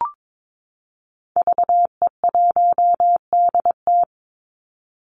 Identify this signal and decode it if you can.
Morse code, heard as a keyed tone, about 22 wpm, 700 Hz. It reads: VE1DT
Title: Call Signs Repeated 3x - 22wpm